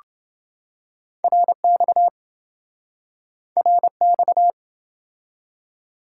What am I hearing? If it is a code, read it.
Morse code, heard as a keyed tone, about 25 wpm, 700 Hz. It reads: RX RX